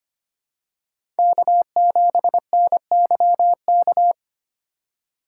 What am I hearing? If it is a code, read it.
Morse code, heard as a keyed tone, about 25 wpm, 700 Hz. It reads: K7NYK
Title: USA Call Signs - 5 Characters - 2x Character Spacing - 25wpm